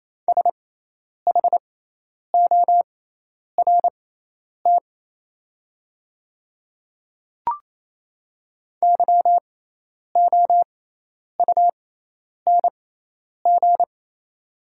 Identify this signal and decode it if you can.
Morse code, heard as a keyed tone, about 28 wpm, spaced as for 10 wpm, 700 Hz. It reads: SHORT YOUNG
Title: 6x Character Spacing - Top 500 Words - 28wpm